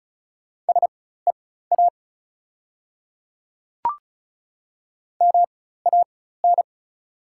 Morse code, heard as a keyed tone, about 36 wpm, spaced as for 16 wpm, 690 Hz. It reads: SEA MAN
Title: 4x Character Spacing - Top 500 Words - 35wpm